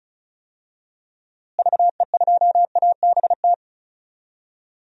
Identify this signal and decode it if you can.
Morse code, heard as a keyed tone, about 34 wpm, 700 Hz. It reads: VE2ABT
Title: DXCC-100 Call Signs - 6 Characters - Repeated 3x - 35wpm